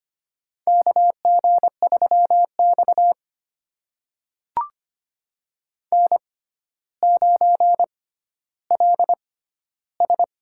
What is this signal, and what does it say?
Morse code, heard as a keyed tone, about 25 wpm, spaced as for 13 wpm, 700 Hz. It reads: KG3X N9LS
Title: USA Call Signs - 4 Characters - 6x Character Spacing - 25wpm